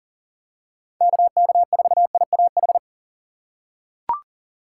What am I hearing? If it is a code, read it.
Morse code, heard as a keyed tone, about 40 wpm, 700 Hz. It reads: KK4IAH
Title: USA Call Signs - 3x Character Spacing - 40wpm